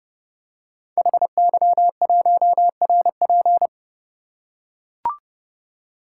Morse code, heard as a keyed tone, about 28 wpm, 700 Hz. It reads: HY1RP
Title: Code Groups - 5 Letter:Numbers - 30wpm